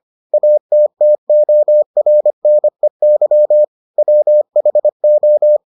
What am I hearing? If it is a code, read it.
Morse code, heard as a keyed tone, about 25 wpm, 600 Hz. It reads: ATTORNEY WHO